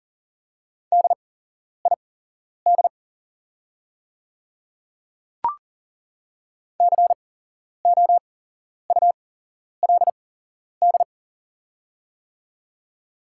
Morse code, heard as a keyed tone, about 41 wpm, spaced as for 11 wpm, 700 Hz. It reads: DID COULD
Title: 8x Character Spacing - Top 100 Words - 40wpm